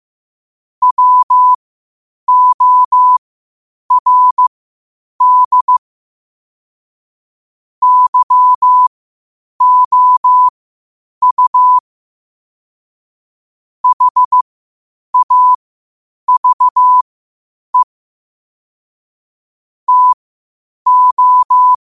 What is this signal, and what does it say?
Morse code, heard as a keyed tone, about 15 wpm, spaced as for 8 wpm, 1000 Hz. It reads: WORD YOU HAVE TO
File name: qotdfiles / 8wpm